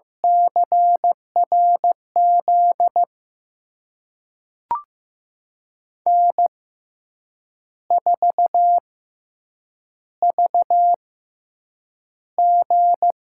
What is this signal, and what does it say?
Morse code, heard as a keyed tone, about 15 wpm, spaced as for 7 wpm, 700 Hz. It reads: CRZ N4VG
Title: USA Call Signs - 5 Characters - 6x Character Spacing - 15wpm